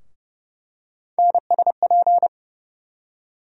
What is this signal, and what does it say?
Morse code, heard as a keyed tone, about 30 wpm, 700 Hz. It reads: NSP
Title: USA Call Sign Postfixes - 3 Letters 30wpm